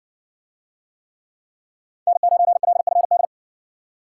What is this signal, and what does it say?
Morse code, heard as a keyed tone, about 60 wpm, 700 Hz. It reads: N9LFD